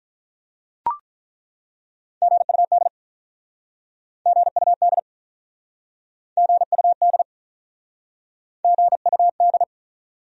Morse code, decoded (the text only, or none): GUD GUD GUD GUD